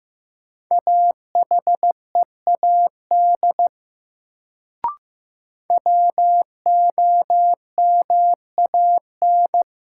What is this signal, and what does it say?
Morse code, heard as a keyed tone, about 15 wpm, 700 Hz. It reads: AHEAD WOMAN
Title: Sentences from Top 1100 Words - New Word Drill Alt - 15wpm